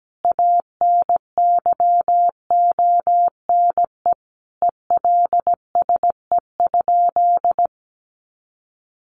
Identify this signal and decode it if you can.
Morse code, heard as a keyed tone, about 17 wpm, 700 Hz. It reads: ANYONE ELSE?
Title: Sentences from Top 900 Words 17wpm